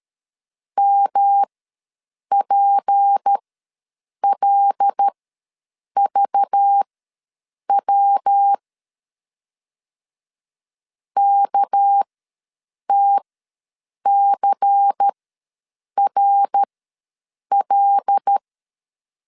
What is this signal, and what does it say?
Morse code, heard as a keyed tone, about 13 wpm, spaced as for 7 wpm, 800 Hz. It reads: MPLVW KTCRL
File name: LEZIONI / 6 PKX0 / 6 PK ws 12wpm 3116